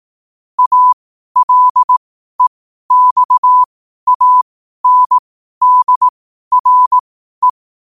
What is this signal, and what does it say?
Morse code, heard as a keyed tone, about 18 wpm, spaced as for 12 wpm, 1000 Hz. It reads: ALEXANDRE
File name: qotdfiles / 12wpm